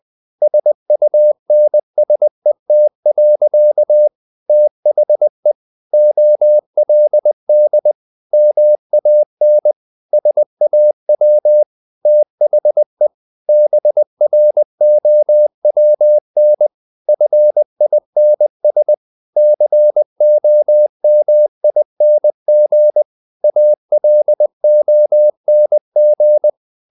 Morse code, heard as a keyed tone, about 20 wpm, 600 Hz. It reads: SUNSET. THE OLD MAN SAW THE BROWN FINS COMING ALONG
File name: the old man and the sea part 2 20wpm